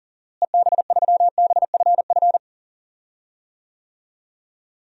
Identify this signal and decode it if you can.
Morse code, heard as a keyed tone, about 40 wpm, 700 Hz. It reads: EB3BFF
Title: DXCC-100 Call Signs - 6 Characters - Repeated 3x - 40wpm